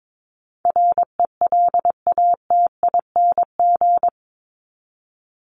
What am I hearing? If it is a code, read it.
Morse code, heard as a keyed tone, about 22 wpm, 700 Hz. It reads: RELATING
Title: ING Suffix - Top 100 Words with Base Words 22wpm